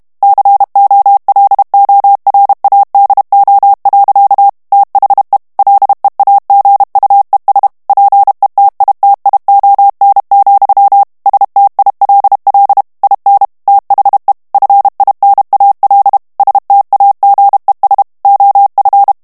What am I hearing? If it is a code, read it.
Morse code, heard as a keyed tone, about 32 wpm, 800 Hz. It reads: COLORADO. THE LEAGUES PETITION, STILL IN THE FINAL STAGES OF
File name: ANGLAIS texte clair 3 vitesse 30 mots minute